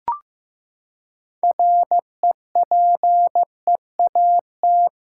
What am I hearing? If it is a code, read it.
Morse code, heard as a keyed tone, about 16 wpm, 700 Hz. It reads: REPEAT
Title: Sentences from Top 1600 Words 15wpm